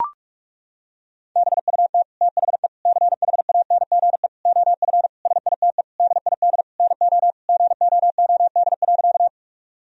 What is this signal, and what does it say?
Morse code, heard as a keyed tone, about 41 wpm, 700 Hz. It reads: BUT THE CHANGE OF SITE DID NO GOOD.
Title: Sentences from Top 1000 Words - Review Part 4 - 45wpm